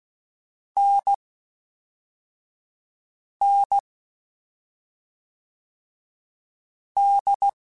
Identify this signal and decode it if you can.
Morse code, heard as a keyed tone, about 16 wpm, 800 Hz. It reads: N N D
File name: Difficult Combos / Difficult Combos-mp3s / awj1 ndb6 gz7 @5